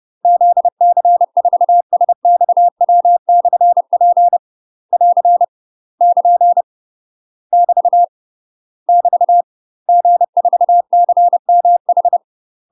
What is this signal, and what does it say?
Morse code, heard as a keyed tone, about 30 wpm, 700 Hz. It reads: ZC4SXW/P <AR> <KN> <BT> <BT> G4CMH
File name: UK MASTER QSOs TRAINING / 30WPM / UK MASTER QSOs 0023